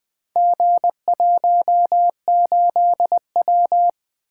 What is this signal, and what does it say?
Morse code, heard as a keyed tone, about 20 wpm, 700 Hz. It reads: G18W
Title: Four Letter-Number Combinations - 20wpm